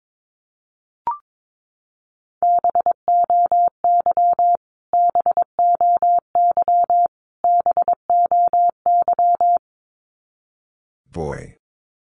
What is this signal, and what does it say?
Morse code, heard as a keyed tone, about 22 wpm, 700 Hz. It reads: BOY BOY BOY
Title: Top 200 Words Repeated 22wpm